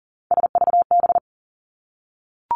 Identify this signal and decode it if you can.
Morse code, heard as a keyed tone, about 40 wpm, 700 Hz. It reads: SVB
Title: Three Challenging Letter-Number Combinations 40wpm